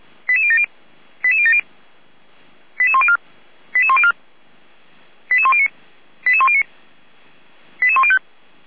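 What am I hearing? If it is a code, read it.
Five-tone selective call (ZVEI), heard as quick runs of five tones, tones of about 70 ms. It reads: 88070 88070 88104 88104 88109 88109 88105
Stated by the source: ZVEI five-tone selective call received on an SDR.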